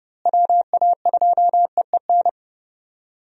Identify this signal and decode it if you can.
Morse code, heard as a keyed tone, about 30 wpm, 700 Hz. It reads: WA2EEN
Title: USA Call Signs - 6 Characters - Repeated 3x - 30wpm